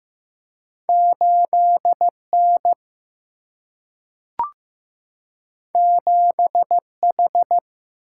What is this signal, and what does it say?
Morse code, heard as a keyed tone, about 15 wpm, 700 Hz. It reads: 8N 7H
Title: Single-Letter Single-Number 15wpm